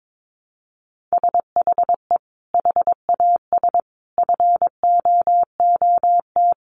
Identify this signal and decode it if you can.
Morse code, heard as a keyed tone, about 22 wpm, 700 Hz. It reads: SHE HAS FOOT
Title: Sentences from Top 1500 Words 22wpm